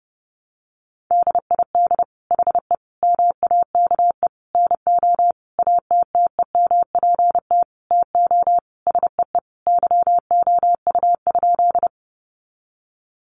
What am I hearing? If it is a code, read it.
Morse code, heard as a keyed tone, about 30 wpm, 700 Hz. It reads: DID HE MAKE NO ATTEMPT TO SEE YOU?
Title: Sentences from Top 1000 Words 30wpm